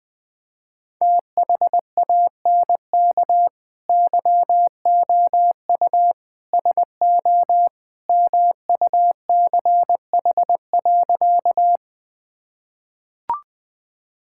Morse code, heard as a keyed tone, about 20 wpm, 700 Hz. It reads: THANK YOU SO MUCH.
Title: Sentences with 2-Word Phrases - 20wpm